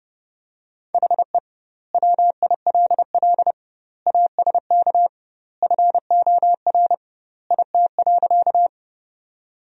Morse code, heard as a keyed tone, about 32 wpm, 700 Hz. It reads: HE WILL ASK FOR IT.
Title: 2x Word-Spacing - Sentences from Top 200 Words - 30wpm